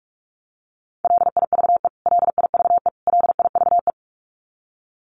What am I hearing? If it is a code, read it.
Morse code, heard as a keyed tone, about 44 wpm, 700 Hz. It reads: LIVE LIVE LIVE
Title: Top 500 Words Repeated 45wpm